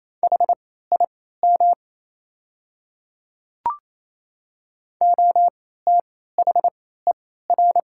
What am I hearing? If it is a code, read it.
Morse code, heard as a keyed tone, about 28 wpm, spaced as for 16 wpm, 700 Hz. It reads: HIM OTHER
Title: 3x Character Spacing - Top 100 Words - 28wpm